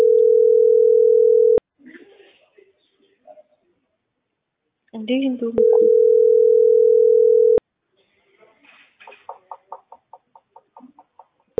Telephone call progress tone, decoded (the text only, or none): ringback tone